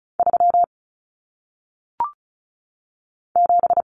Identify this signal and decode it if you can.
Morse code, heard as a keyed tone, about 36 wpm, 700 Hz. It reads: 3 7
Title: Numbers - 1-Digit 35wpm